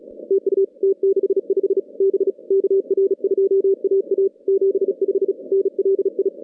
Morse code, heard as a keyed tone, about 36 wpm, 410 Hz. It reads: TU T65 B KR2AA 75 NLI
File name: contests / CW-SS-k5tr- -2200